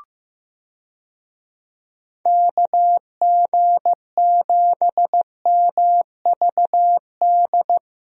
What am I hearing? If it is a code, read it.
Morse code, heard as a keyed tone, about 15 wpm, 700 Hz. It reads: KG7MVD
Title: USA Call Signs - 6 Characters - Repeated 2x - 15wpm